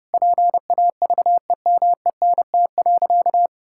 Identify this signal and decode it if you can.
Morse code, heard as a keyed tone, about 30 wpm, 700 Hz. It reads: PAVEMENT.
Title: Pangram Challenge v2 - 30wpm